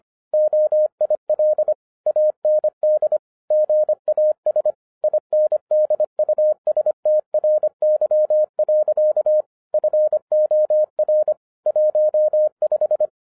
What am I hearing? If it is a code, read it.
Morse code, heard as a keyed tone, about 26 wpm, 610 Hz. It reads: OIL AND GAS INDUSTRY. FOR 15